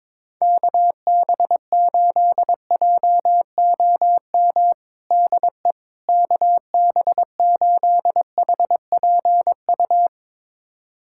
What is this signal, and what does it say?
Morse code, heard as a keyed tone, about 22 wpm, 700 Hz. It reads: KB8JOM DE KB8HPU
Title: USA Callsign DE Callsign - 6 Characters - 22wpm